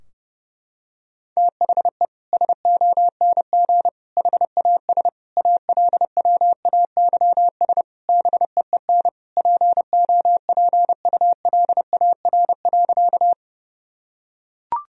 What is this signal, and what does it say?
Morse code, heard as a keyed tone, about 32 wpm, 700 Hz. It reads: THE SONG HAS ALWAYS BEEN POPULAR.
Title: Sentences from Top 1000 Words - Review Part 3 - 30wpm